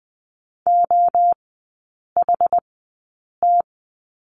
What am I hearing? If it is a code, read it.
Morse code, heard as a keyed tone, about 20 wpm, spaced as for 8 wpm, 700 Hz. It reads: OHT
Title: 093 Lesson 16e - Letter c - Three Characters 20wpm